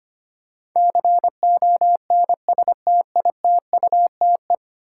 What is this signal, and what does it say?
Morse code, heard as a keyed tone, about 25 wpm, 700 Hz. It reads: CONSTITUTE
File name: 10 Letter Words - First 3 Letters Given - 25wpm